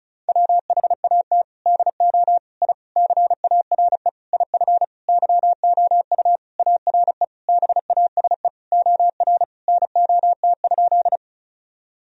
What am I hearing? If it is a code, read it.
Morse code, heard as a keyed tone, about 34 wpm, 700 Hz. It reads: WHAT DO I CARE IF YOU ARE BASE OR NOT?
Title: Sentences from Top 1000 Words - Review Part 1 - 35wpm